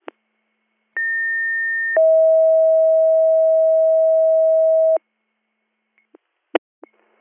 Two-tone page, heard as a one-second tone, then a three-second tone, about 1800 then 640 Hz. It is heard from 1.0 to 5.0 s.